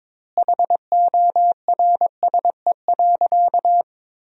Morse code, heard as a keyed tone, about 21 wpm, 700 Hz. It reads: HORSE.
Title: 4x Word-Spacing - Sentences from Top 300 Words - 22wpm